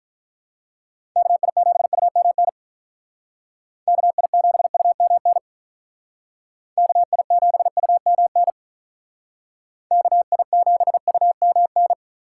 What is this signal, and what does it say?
Morse code, heard as a keyed tone, about 45 wpm, 700 Hz. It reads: KI7UMN KI7UMN KI7UMN KI7UMN